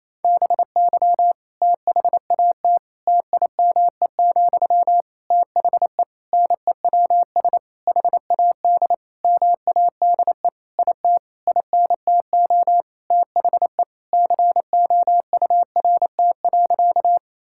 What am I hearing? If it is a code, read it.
Morse code, heard as a keyed tone, about 28 wpm, 700 Hz. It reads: BY THAT TIME, THE NEWS HAD MADE IT INTO THE COURT.
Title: Sentences from Top 1600 Words 28wpm